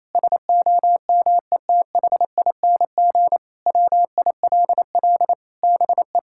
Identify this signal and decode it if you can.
Morse code, heard as a keyed tone, about 28 wpm, 700 Hz. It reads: SOMETHING WILL BE
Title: Sentences from Top 1600 Words 28wpm